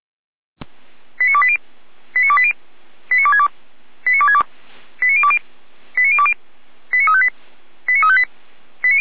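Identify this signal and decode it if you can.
Five-tone selective call (ZVEI), heard as quick runs of five tones, tones of about 70 ms. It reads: 89280 89280 89262 89262 89020 89020 89358 89358
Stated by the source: ZVEI five-tone selective call received on an SDR, logged by the Feuerwehr Orlamünde fire brigade server.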